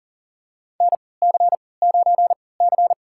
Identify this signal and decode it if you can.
Morse code, heard as a keyed tone, about 40 wpm, spaced as for 23 wpm, 700 Hz. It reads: NC9C
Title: USA Call Signs - 4 Characters - 3x Character Spacing - 40wpm